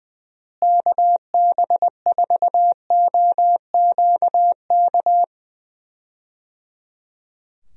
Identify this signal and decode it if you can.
Morse code, heard as a keyed tone, about 20 wpm, 700 Hz. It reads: KB4OQK
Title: USA Call Signs - 6 Characters - Repeated 3x - 20wpm